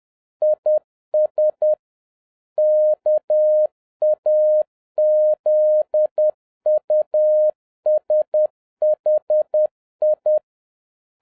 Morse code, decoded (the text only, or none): IS KAZUSHI